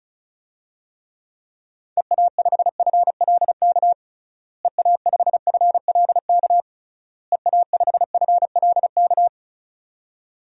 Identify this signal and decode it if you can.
Morse code, heard as a keyed tone, about 36 wpm, 700 Hz. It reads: EA5FLK EA5FLK EA5FLK
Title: DXCC-100 Call Signs - 6 Characters - Repeated 3x - 35wpm